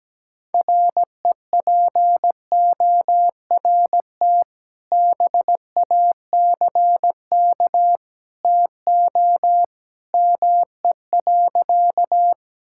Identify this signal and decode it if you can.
Morse code, heard as a keyed tone, about 17 wpm, 700 Hz. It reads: REPORT BACK TO ME.
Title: Sentences from Top 1000 Words - Review Part 6 - 17wpm